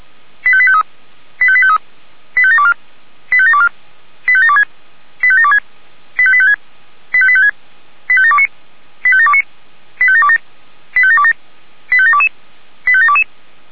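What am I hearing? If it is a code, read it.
Five-tone selective call (ZVEI), heard as quick runs of five tones, tones of about 70 ms. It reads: 85853 85853 85724 85724 85726 85726 85856 85856 85729 85729 85727 85727 85722 85722
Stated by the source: ZVEI five-tone call received by SDR.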